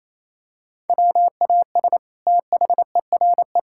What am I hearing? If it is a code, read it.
Morse code, heard as a keyed tone, about 28 wpm, 700 Hz. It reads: WAS THERE
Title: Sentences from Top 1000 Words - Review Part 6 - 28wpm